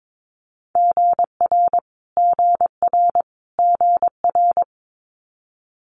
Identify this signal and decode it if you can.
Morse code, heard as a keyed tone, about 22 wpm, 700 Hz. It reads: GR GR GR
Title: Multi-letter Phonemes Repeated 3x 22wpm